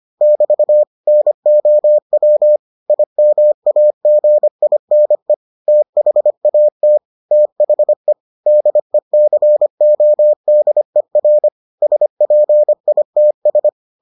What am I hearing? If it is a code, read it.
Morse code, heard as a keyed tone, about 25 wpm, 600 Hz. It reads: <BT> NOW IMAGINE THAT THE DECODER SPITS